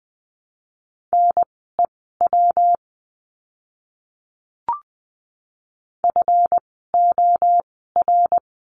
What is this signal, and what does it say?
Morse code, heard as a keyed tone, about 20 wpm, spaced as for 15 wpm, 700 Hz. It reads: NEW FOR
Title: 237 Lesson 38f - Letter q - Words 20wpm